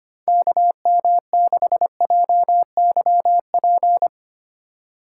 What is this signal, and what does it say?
Morse code, heard as a keyed tone, about 25 wpm, 700 Hz. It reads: KM6JYP
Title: USA Call Signs - 6 Characters - 3x Character Spacing - 25wpm